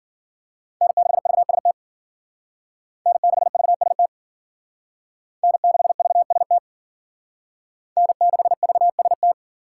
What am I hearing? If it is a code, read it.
Morse code, heard as a keyed tone, about 51 wpm, 700 Hz. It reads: N6VST N6VST N6VST N6VST